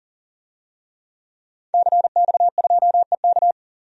Morse code, heard as a keyed tone, about 37 wpm, 700 Hz. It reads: CX2EK
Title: DXCC-100 Call Signs - 5 Characters - 40wpm